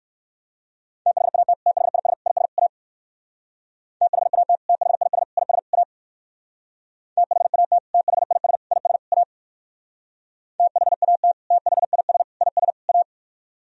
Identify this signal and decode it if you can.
Morse code, heard as a keyed tone, about 60 wpm, 700 Hz. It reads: THAT THIS IS A THAT THIS IS A THAT THIS IS A THAT THIS IS A